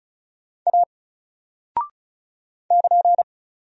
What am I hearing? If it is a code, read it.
Morse code, heard as a keyed tone, about 36 wpm, 700 Hz. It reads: A <KN>
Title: Single Character (Letter:Number:Punctuation:Prosign) 35wpm